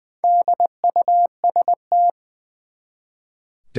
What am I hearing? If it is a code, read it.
Morse code, heard as a keyed tone, about 20 wpm, 700 Hz. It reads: DUST
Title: Four Letter Words - 20wpm